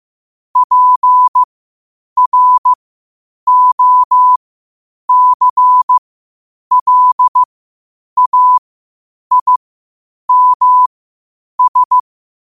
Morse code, heard as a keyed tone, about 15 wpm, spaced as for 8 wpm, 1000 Hz. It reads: PROCLAIMS
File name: qotdfiles / 8wpm